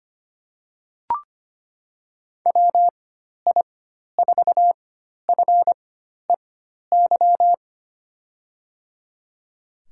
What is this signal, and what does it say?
Morse code, heard as a keyed tone, about 25 wpm, spaced as for 12 wpm, 700 Hz. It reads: WI4FEY
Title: USA Call Signs - 6 Characters - 4x Character Spacing - 25wpm